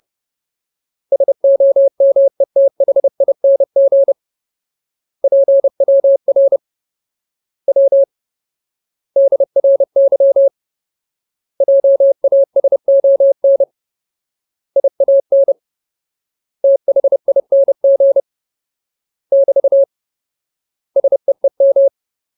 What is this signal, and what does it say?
Morse code, heard as a keyed tone, about 30 wpm, 550 Hz. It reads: SOMETHING PWR W DRY JASON IAN THING <BT> SEEM
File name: Ultimate Word List / 30WPM Ultimate